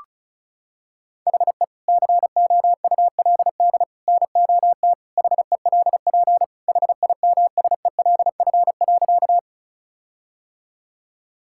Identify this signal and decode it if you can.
Morse code, heard as a keyed tone, about 36 wpm, 700 Hz. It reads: HE COULD NOT HELP HIMSELF.
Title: Sentences from Top 1600 Words 35wpm